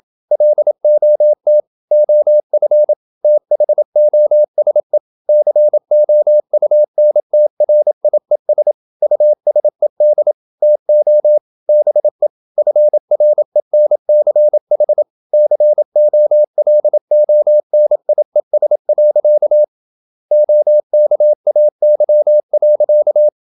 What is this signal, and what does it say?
Morse code, heard as a keyed tone, about 27 wpm, 600 Hz. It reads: LOT OF THOSE COUNTRIES USED TO BE FRENCH COLONIES. OKAY.
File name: Never 0027wpm